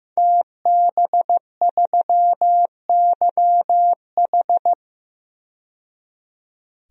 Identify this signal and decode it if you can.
Morse code, heard as a keyed tone, about 15 wpm, 700 Hz. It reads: TB3YH
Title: DXCC-100 Call Signs - 5 Characters - 15wpm